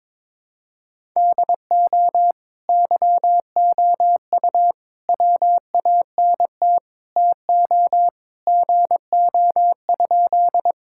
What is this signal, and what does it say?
Morse code, heard as a keyed tone, about 22 wpm, 700 Hz. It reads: DO YOU WANT TO GO?
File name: practice / 22wpm / Sentences with 3-Word Phrases - 22wpm